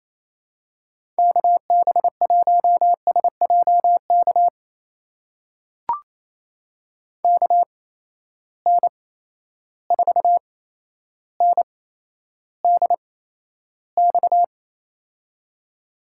Morse code, heard as a keyed tone, about 28 wpm, spaced as for 12 wpm, 700 Hz. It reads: KB1SJK KN4NDX
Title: USA Call Signs - 6 Characters - 8x Character Spacing - 28wpm